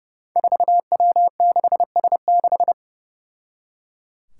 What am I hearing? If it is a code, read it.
Morse code, heard as a keyed tone, about 30 wpm, 700 Hz. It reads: 4W6S6
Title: Five Challenging Letter-Number Combinations 30wpm